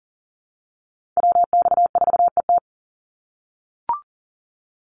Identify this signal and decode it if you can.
Morse code, heard as a keyed tone, about 39 wpm, 700 Hz. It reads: WX4ET
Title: NA SOTA Chaser Callsigns 40wpm